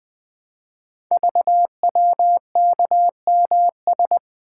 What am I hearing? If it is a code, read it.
Morse code, heard as a keyed tone, about 20 wpm, 700 Hz. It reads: VWKMS
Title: Code Groups - 5 Letters - 20wpm